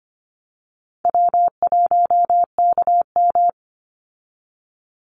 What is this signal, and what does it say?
Morse code, heard as a keyed tone, about 25 wpm, 700 Hz. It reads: W1KM
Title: Easy Call Signs Repeated 3x - 25wpm